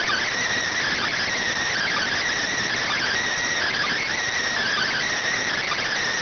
Emergency Alert System (SAME) tones (header bursts), heard as harsh buzzing bursts. It is heard from 0.6 to 5.5 s.